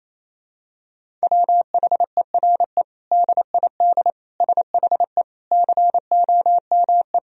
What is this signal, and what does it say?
Morse code, heard as a keyed tone, about 28 wpm, 700 Hz. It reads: WHERE DID SHE COME
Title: Sentences from Top 200 Words and Abbreviations 28wpm